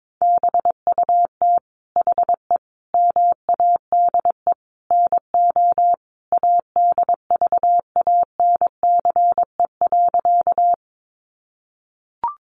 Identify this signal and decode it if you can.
Morse code, heard as a keyed tone, about 22 wpm, 700 Hz. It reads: BUT HE MADE NO ADVANCE.
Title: Sentences from Top 1400 Words 22wpm